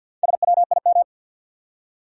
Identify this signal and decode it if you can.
Morse code, heard as a keyed tone, about 53 wpm, 700 Hz. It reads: SWIM
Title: Four Letter Words - 50wpm